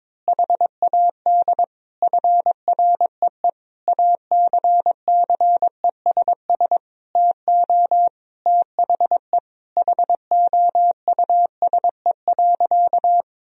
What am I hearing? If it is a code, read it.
Morse code, heard as a keyed tone, about 22 wpm, 700 Hz. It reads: HAD FREE ACCESS TO THE HOUSE.
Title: Sentences from Top 1000 Words - Review Part 4 - 22wpm